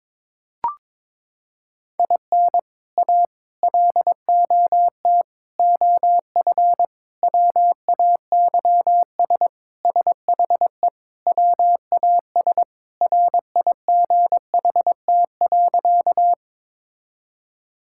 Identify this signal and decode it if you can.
Morse code, heard as a keyed tone, about 22 wpm, 700 Hz. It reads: IN A LOT OF WAYS SHE WAS RIGHT.
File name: practice / 22wpm / Sentences with 4-Word Phrases - 22wpm